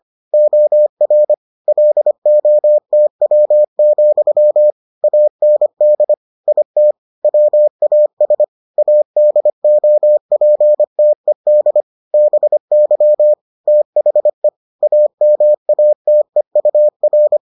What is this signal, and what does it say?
Morse code, heard as a keyed tone, about 25 wpm, 600 Hz. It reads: OR LOTW, AND IT WAS ADOPTED BY THE AMATEUR